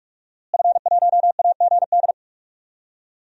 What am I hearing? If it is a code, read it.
Morse code, heard as a keyed tone, about 46 wpm, 700 Hz. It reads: F1AGD
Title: DXCC-100 Call Signs - 5 Characters - Repeated 3x - 45wpm